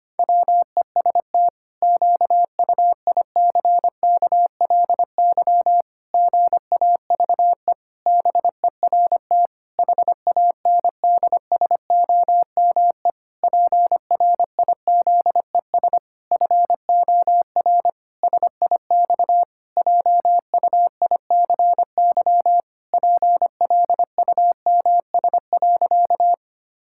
Morse code, heard as a keyed tone, about 25 wpm, 700 Hz. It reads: WEST QUICKLY GAVE BERT HANDSOME PRIZES FOR SIX JUICY PLUMS.
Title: Pangram Challenge v2 - 25wpm